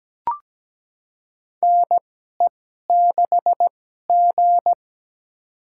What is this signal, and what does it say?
Morse code, heard as a keyed tone, about 17 wpm, spaced as for 12 wpm, 700 Hz. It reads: NE6G
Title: USA Call Signs - 4 Characters - 2x Character Spacing - 17wpm